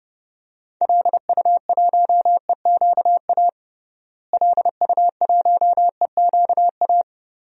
Morse code, heard as a keyed tone, about 32 wpm, 700 Hz. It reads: LU1EQA LU1EQA
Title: DXCC-100 Call Signs - 6 Characters - Repeated 3x - 30wpm